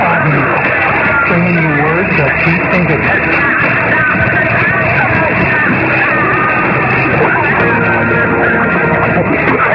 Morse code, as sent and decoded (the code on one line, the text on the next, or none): none